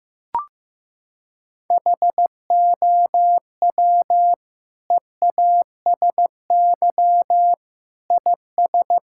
Morse code, heard as a keyed tone, about 15 wpm, 700 Hz. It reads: HOW EASY IS
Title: Sentences from Top 1000 Words - Review Part 6 - 15wpm